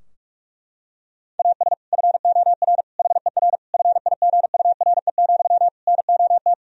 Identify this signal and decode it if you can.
Morse code, heard as a keyed tone, about 47 wpm, 700 Hz. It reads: AS FOR HER FIGURE, NOT
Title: Sentences from Top 1000 Words - Review Part 2 - 45wpm